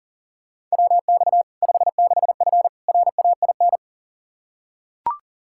Morse code, heard as a keyed tone, about 40 wpm, 700 Hz. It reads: WX 56F RAIN
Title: Practice QSO - Exchange 2 - Info - 40wpm